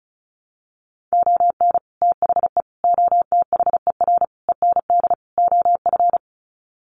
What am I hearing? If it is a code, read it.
Morse code, heard as a keyed tone, about 34 wpm, 700 Hz. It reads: ON THE OTHER END OF
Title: Sets of 5 Words Encore 35wpm